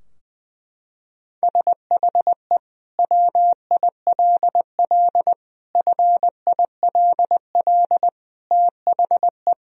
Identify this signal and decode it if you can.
Morse code, heard as a keyed tone, about 19 wpm, 700 Hz. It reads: SHE WILL FILL THE
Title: Sentences from Top 1000 Words - Review Part 3 - 20wpm